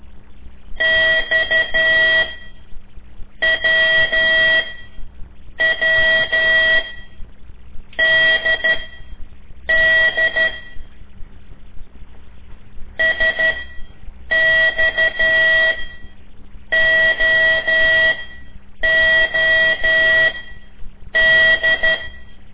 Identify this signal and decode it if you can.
Morse code, heard as a keyed tone, about 12 wpm, spaced as for 6 wpm, 1950 Hz. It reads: XWWDD SXOOD
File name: ik1weg2 / CORSI CW / lez5-1